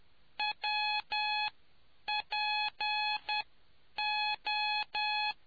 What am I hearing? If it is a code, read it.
Morse code, heard as a keyed tone, about 10 wpm, spaced as for 8 wpm, 840 Hz. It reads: WPO